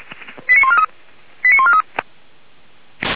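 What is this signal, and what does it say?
Five-tone selective call (ZVEI), heard as quick runs of five tones, tones of about 70 ms. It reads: 89143 89143
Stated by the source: ZVEI five-tone selective call received by SDR, recorded on the Feuerwehr Orlamünde server.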